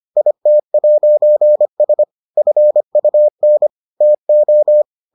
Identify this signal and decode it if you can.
Morse code, heard as a keyed tone, about 25 wpm, 600 Hz. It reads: IT'S FUN TO